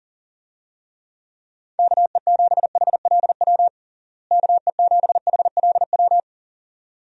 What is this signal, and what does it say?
Morse code, heard as a keyed tone, about 40 wpm, 700 Hz. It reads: KE7HLW KE7HLW
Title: USA Call Signs - 6 Characters - Repeated 2x - 40wpm